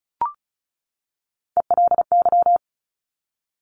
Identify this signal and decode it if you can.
Morse code, heard as a keyed tone, about 36 wpm, 700 Hz. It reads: ELY
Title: Top Three Letter Combinations in Words 35wpm